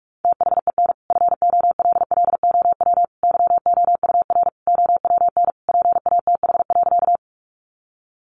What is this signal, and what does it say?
Morse code, heard as a keyed tone, about 44 wpm, 700 Hz. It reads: THEN FOLLOW YOUR OWN PATH.
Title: Sentences from Top 1400 Words 45wpm